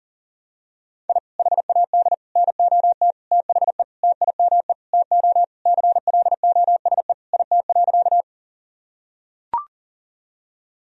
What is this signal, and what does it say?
Morse code, heard as a keyed tone, about 40 wpm, 700 Hz. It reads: I HAD NOT THE TIME TO CLOSE IT.
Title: Sentences from Top 1000 Words - Review Part 1 - 40wpm